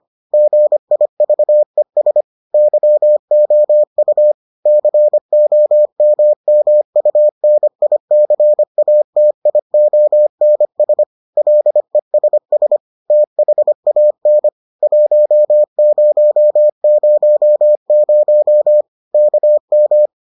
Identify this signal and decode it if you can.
Morse code, heard as a keyed tone, about 25 wpm, 600 Hz. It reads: GIVES YOU COMMUNICATIONS LESS THAN 1000 KM